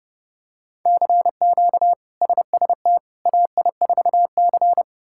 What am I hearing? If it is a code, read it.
Morse code, heard as a keyed tone, about 32 wpm, 700 Hz. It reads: CQ SST AI4C
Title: K1USN SST QSO Practice - 30wpm